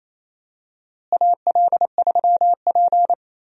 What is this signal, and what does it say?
Morse code, heard as a keyed tone, about 28 wpm, 700 Hz. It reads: AL3P
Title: USA Call Signs - 4 Characters - Repeated 3x - 28wpm